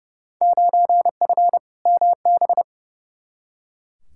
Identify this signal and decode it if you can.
Morse code, heard as a keyed tone, about 30 wpm, 700 Hz. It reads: 9F MB
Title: Field Day Exchanges Encore - 30wpm